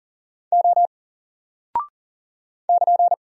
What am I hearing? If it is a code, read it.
Morse code, heard as a keyed tone, about 40 wpm, 700 Hz. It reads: O <KN>
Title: Single Character (Letter:Number:Punctuation:Prosign) Alt 40wpm